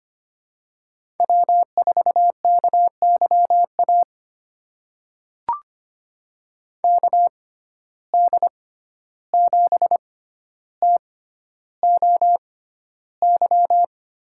Morse code, heard as a keyed tone, about 25 wpm, spaced as for 12 wpm, 700 Hz. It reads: W4KYA KD7TOY
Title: USA Call Signs - 6x Character Spacing - 25wpm